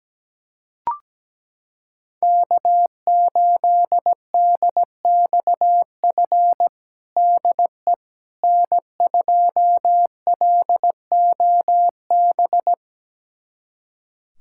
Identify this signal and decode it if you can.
Morse code, heard as a keyed tone, about 17 wpm, 700 Hz. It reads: K8DXF DE N2LOB
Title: USA Callsign DE Callsign - 5 Characters - 17wpm